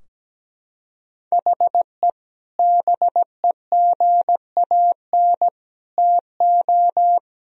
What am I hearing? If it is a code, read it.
Morse code, heard as a keyed tone, about 17 wpm, 700 Hz. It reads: HE BEGAN TO
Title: Sentences from Top 1000 Words - Review Part 4 - 17wpm